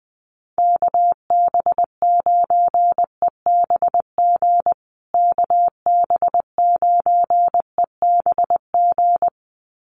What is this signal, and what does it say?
Morse code, heard as a keyed tone, about 20 wpm, 700 Hz. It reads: KB9EBG KB9EBG
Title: USA Call Signs Repeated 2x 20wpm